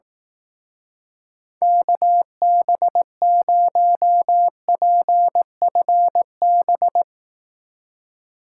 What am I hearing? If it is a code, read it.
Morse code, heard as a keyed tone, about 18 wpm, 700 Hz. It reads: KB0PFB
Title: Speed Racing - USA Call Signs 15wpm